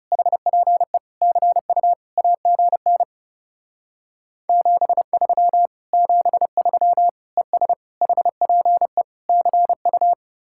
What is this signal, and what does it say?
Morse code, heard as a keyed tone, about 36 wpm, 700 Hz. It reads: HPE CU AGN 73 73 ES HPE CU
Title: Speed-Racing Common QSO Phrases Encore - 30wpm